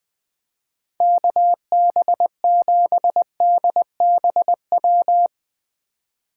Morse code, heard as a keyed tone, about 20 wpm, 700 Hz. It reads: KB7DBW
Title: USA Call Signs - 6 Characters - 8x Character Spacing - 20wpm